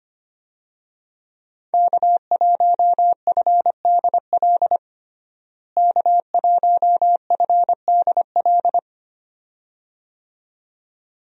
Morse code, heard as a keyed tone, about 25 wpm, 700 Hz. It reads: K1FDL K1FDL
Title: USA Call Signs - 5 Characters - Repeated 2x - 25wpm